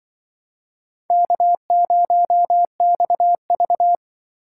Morse code, heard as a keyed tone, about 24 wpm, 700 Hz. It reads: K0XV
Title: Speed Racing - USA 1x2 Call Signs 20wpm